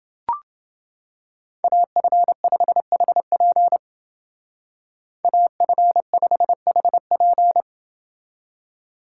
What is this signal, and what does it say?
Morse code, heard as a keyed tone, about 30 wpm, 700 Hz. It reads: AF5HP AF5HP